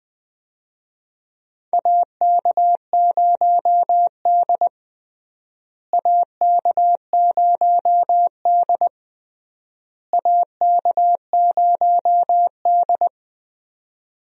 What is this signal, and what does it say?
Morse code, heard as a keyed tone, about 20 wpm, 700 Hz. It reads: AK0D AK0D AK0D
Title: USA Call Signs - 4 Characters - Repeated 3x - 20wpm